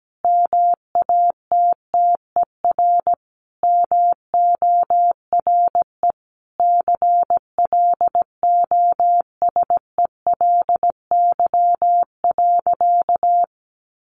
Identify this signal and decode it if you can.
Morse code, heard as a keyed tone, about 17 wpm, 700 Hz. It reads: MATTER MORE CLOSELY.
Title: Sentences from Top 1400 Words 17wpm